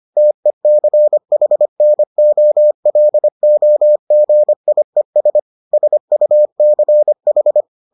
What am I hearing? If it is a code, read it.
Morse code, heard as a keyed tone, about 25 wpm, 600 Hz. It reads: TECHNOLOGIES SUCH